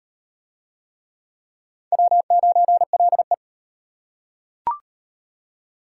Morse code, heard as a keyed tone, about 33 wpm, 700 Hz. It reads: W9LE